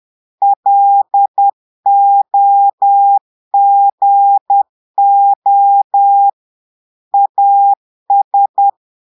Morse code, heard as a keyed tone, about 10 wpm, 800 Hz. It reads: LOGO AS
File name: portugais texte en clair 2 vitesse 10 mots minutes